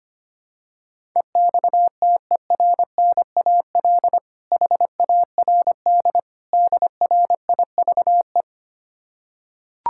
Morse code, heard as a keyed tone, about 25 wpm, 700 Hz. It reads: EXTERNAL HARD DRIVE
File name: practice / 25wpm / Difficult Words with Hints - 25wpm